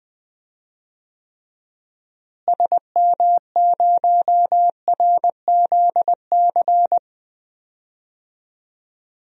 Morse code, heard as a keyed tone, about 20 wpm, 700 Hz. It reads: SM0RZC
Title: DXCC-100 Call Signs - 6 Characters - Repeated 2x - 20wpm